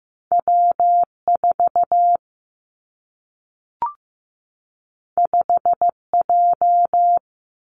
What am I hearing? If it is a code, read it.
Morse code, heard as a keyed tone, about 15 wpm, 700 Hz. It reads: W 4 5 J
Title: Two Challenging Letter-Number Combinations 15wpm